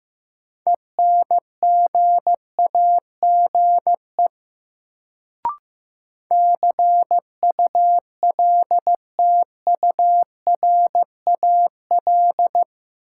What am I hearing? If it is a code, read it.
Morse code, heard as a keyed tone, about 15 wpm, 700 Hz. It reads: ENGAGE CULTURAL
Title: Sentences from Top 1500 Words - New Word Drill - 15wpm